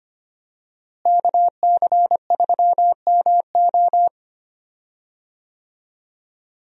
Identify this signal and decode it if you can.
Morse code, heard as a keyed tone, about 25 wpm, 700 Hz. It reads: KC3MO
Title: USA Call Signs - 5 Characters - Repeated 3x - 25wpm